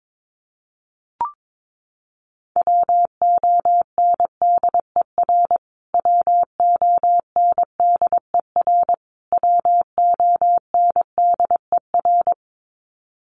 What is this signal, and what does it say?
Morse code, heard as a keyed tone, about 22 wpm, 700 Hz. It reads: WONDER WONDER WONDER
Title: Top 500 Words Repeated 22wpm